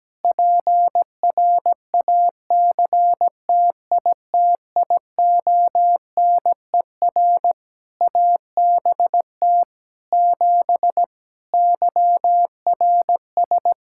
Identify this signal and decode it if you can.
Morse code, heard as a keyed tone, about 17 wpm, 700 Hz. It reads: PRACTITIONER ABT 7 YRS
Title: Practice QSO - Exchange 3 - More Info - 17wpm